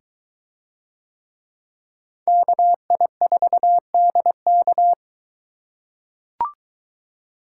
Morse code, heard as a keyed tone, about 21 wpm, 700 Hz. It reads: KI4DK